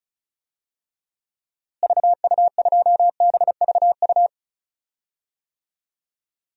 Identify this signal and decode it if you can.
Morse code, heard as a keyed tone, about 34 wpm, 700 Hz. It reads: VU2BVU
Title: DXCC-100 Call Signs - 6 Characters - Repeated 3x - 35wpm